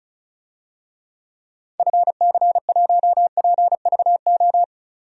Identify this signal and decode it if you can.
Morse code, heard as a keyed tone, about 36 wpm, 700 Hz. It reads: FC1PVO
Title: DXCC-100 Call Signs - 6 Characters - 35wpm